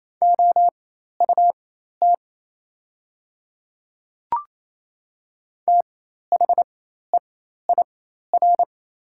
Morse code, heard as a keyed tone, about 28 wpm, spaced as for 13 wpm, 700 Hz. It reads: OUT THEIR
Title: 4x Character Spacing - Top 100 Words - 28wpm